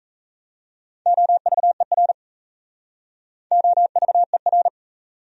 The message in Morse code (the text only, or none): OVER OVER